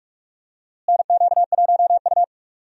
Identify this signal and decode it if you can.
Morse code, heard as a keyed tone, about 46 wpm, 700 Hz. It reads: NQ1U
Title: USA Call Signs - 4 Characters - Repeated 2x - 45wpm